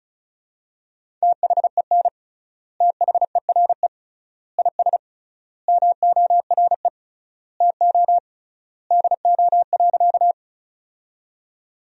Morse code, heard as a keyed tone, about 36 wpm, 700 Hz. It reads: THEN THERE IS MORE TO DO.
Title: 3x Word-Spacing - Sentences from Top 100 Words - 35wpm